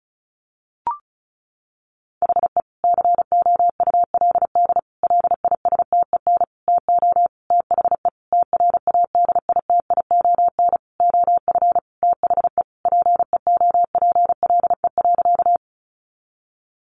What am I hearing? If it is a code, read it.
Morse code, heard as a keyed tone, about 34 wpm, 700 Hz. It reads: HE COULD LISTEN TO THE TRADITION OF THE PEOPLE.
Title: Sentences from Top 1500 Words 35wpm